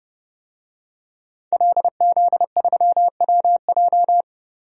Morse code, heard as a keyed tone, about 30 wpm, 700 Hz. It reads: LZ3WJ
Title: DXCC-100 Call Signs - 5 Characters - Repeated 2x - 30wpm